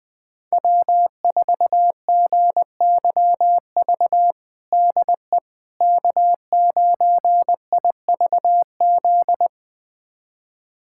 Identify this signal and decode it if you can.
Morse code, heard as a keyed tone, about 20 wpm, 700 Hz. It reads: W4GYV DE K9IVZ
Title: USA Callsign DE Callsign - 5 Characters - 20wpm